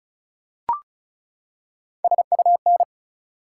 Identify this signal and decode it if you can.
Morse code, heard as a keyed tone, about 34 wpm, 700 Hz. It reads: SUN
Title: Three Letter Words - 35wpm